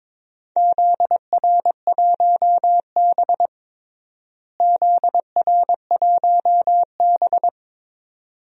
Morse code, heard as a keyed tone, about 21 wpm, 700 Hz. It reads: ZR1B ZR1B
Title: DXCC-100 Call Signs - 4 Characters - Repeated 2x - 22wpm